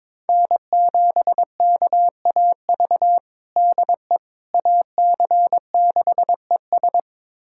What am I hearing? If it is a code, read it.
Morse code, heard as a keyed tone, about 22 wpm, 700 Hz. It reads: N7KAV DE AC6ES
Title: USA Callsign DE Callsign - 5 Characters - 22wpm